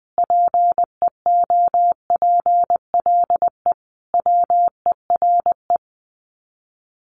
Morse code, heard as a keyed tone, about 20 wpm, 700 Hz. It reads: PEOPLE WERE
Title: Sets of 5 Words Encore2 20wpm